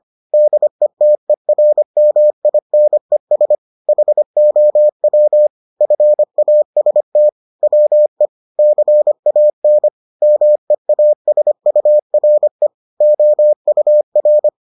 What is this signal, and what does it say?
Morse code, heard as a keyed tone, about 25 wpm, 600 Hz. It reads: DETERMINES HOW FAST WE CAN MEASURE OUR